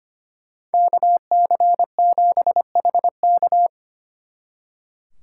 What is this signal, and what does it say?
Morse code, heard as a keyed tone, about 31 wpm, 700 Hz. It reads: KC7HK
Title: Speed Racing - USA 2x2 Call Signs 25wpm